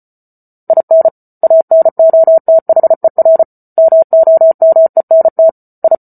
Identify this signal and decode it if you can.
Morse code, heard as a keyed tone, about 34 wpm, 650 Hz. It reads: IN ANOTHER MOMENT I